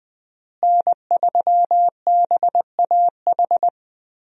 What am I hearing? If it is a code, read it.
Morse code, heard as a keyed tone, about 20 wpm, 700 Hz. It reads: N3BAH
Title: USA Call Signs - 5 Characters - Repeated 2x - 20wpm